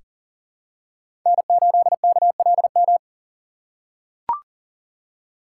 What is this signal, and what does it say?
Morse code, heard as a keyed tone, about 40 wpm, 700 Hz. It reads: N8KLM
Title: USA Call Signs - 5 Characters - 3x Character Spacing - 40wpm